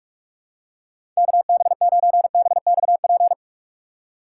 Morse code, heard as a keyed tone, about 47 wpm, 700 Hz. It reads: KB9BXP
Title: USA Call Signs - 6 Characters - 6x Character Spacing - 45wpm